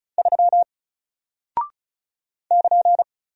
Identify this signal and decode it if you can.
Morse code, heard as a keyed tone, about 36 wpm, 700 Hz. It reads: 3 <KN>
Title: Single Character (Letter:Number:Punctuation:Prosign) Alt 35wpm